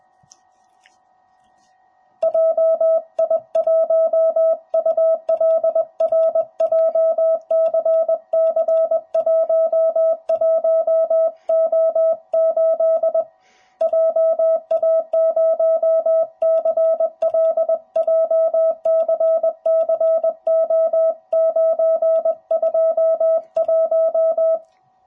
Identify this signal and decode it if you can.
Morse code, heard as a keyed tone, about 20 wpm, 650 Hz. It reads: JI1ULR JCC11O8 JA0CL JCCO921
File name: jh8cbh / 2021cw / cw05b